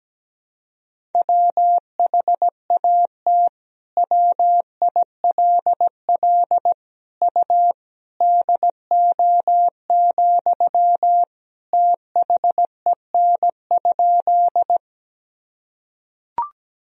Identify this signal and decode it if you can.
Morse code, heard as a keyed tone, about 17 wpm, 700 Hz. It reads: WHAT WILL U DO, THEN?
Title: Sentences from Top 100 Words and Abbreviations 17wpm